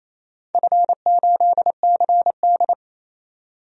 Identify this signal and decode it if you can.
Morse code, heard as a keyed tone, about 28 wpm, 700 Hz. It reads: F8CD
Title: DXCC-100 Call Signs - 4 Characters - 28wpm